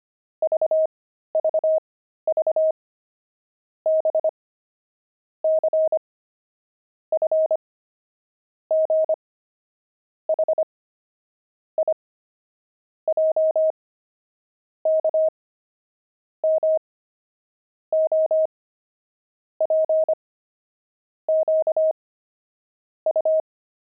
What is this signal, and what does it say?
Morse code, heard as a keyed tone, about 25 wpm, spaced as for 13 wpm, 630 Hz. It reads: VVV B C F G H I J K M O P Q U
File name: pangram-fs-13-cs-25-letters